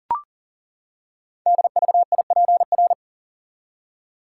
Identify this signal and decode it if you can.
Morse code, heard as a keyed tone, about 40 wpm, 690 Hz. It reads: DVIPR
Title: Code Groups - 5 Letters - 40wpm